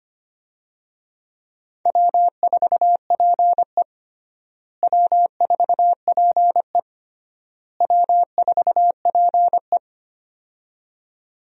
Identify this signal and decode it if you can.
Morse code, heard as a keyed tone, about 25 wpm, 700 Hz. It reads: W4PE W4PE W4PE
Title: USA Call Signs - 4 Characters - Repeated 3x - 25wpm